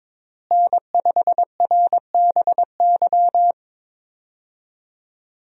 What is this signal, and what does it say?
Morse code, heard as a keyed tone, about 22 wpm, 700 Hz. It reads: N5RBY
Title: USA Call Signs - 5 Characters - Repeated 3x - 22wpm